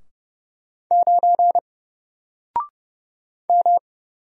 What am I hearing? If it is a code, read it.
Morse code, heard as a keyed tone, about 30 wpm, 700 Hz. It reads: 9 M
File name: Single-Letter-Number-30wpm